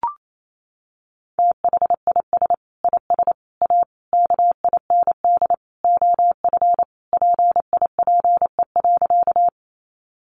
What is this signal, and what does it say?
Morse code, heard as a keyed tone, about 28 wpm, 700 Hz. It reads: THIS IS A KIND OF PIPE.
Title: Sentences from Top 1500 Words 28wpm